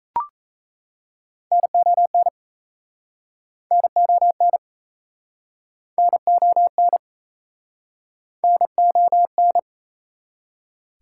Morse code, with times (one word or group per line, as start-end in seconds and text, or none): 1.5-2.3 NON
3.7-4.6 NON
6.0-7.0 NON
8.4-9.6 NON